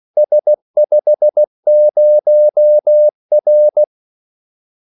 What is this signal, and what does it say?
Morse code, heard as a keyed tone, about 16 wpm, 600 Hz. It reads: S50R
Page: Kaldesignaler 16 wpm